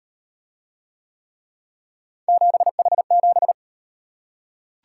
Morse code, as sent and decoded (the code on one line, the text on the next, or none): --... .... --...
7H7